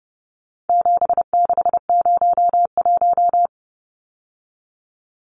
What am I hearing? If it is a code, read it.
Morse code, heard as a keyed tone, about 30 wpm, 700 Hz. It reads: 7601
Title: Numbers - 4-Digit 30wpm